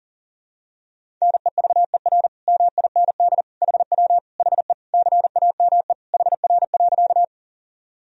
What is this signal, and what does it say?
Morse code, heard as a keyed tone, about 40 wpm, 700 Hz. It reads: NEVER MIND HW HE CAME HR.
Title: Sentences from Top 500 Words and Abbreviations 40wpm